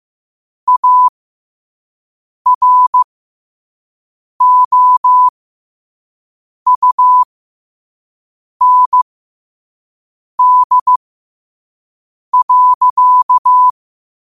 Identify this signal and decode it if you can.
Morse code, heard as a keyed tone, about 15 wpm, spaced as for 5 wpm, 1000 Hz. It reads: AROUND.
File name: qotdfiles / 5wpm